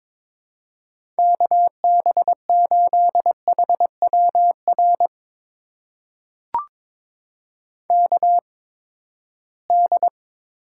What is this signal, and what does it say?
Morse code, heard as a keyed tone, about 22 wpm, spaced as for 15 wpm, 700 Hz. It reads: KB8HWR KD
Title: USA Call Signs - 6 Characters - 8x Character Spacing - 22wpm